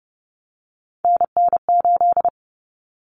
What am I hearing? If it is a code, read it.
Morse code, heard as a keyed tone, about 30 wpm, 700 Hz. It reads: NN8
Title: Common Cut Numbers - 3-Digit 30wpm